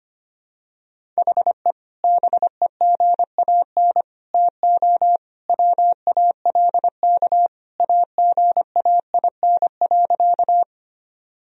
Morse code, heard as a keyed tone, about 25 wpm, 700 Hz. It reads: HE BEGAN TO WALK AGAIN.
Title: Sentences from Top 1000 Words - Review Part 1 - 25wpm